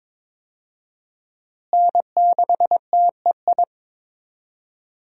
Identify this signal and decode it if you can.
Morse code, heard as a keyed tone, about 22 wpm, 700 Hz. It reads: N6TEI
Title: USA Call Signs - 5 Characters - Repeated 2x - 22wpm